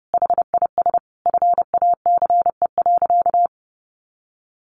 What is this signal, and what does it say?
Morse code, heard as a keyed tone, about 30 wpm, 700 Hz. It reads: HIS FACE.
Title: Sentences from Top 700 Words 30wpm